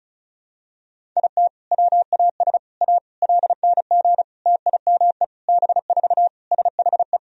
Morse code, heard as a keyed tone, about 36 wpm, 700 Hz. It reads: IT WAS A LNG TIME B4 SHE
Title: Sentences from Top 500 Words and Abbreviations 35wpm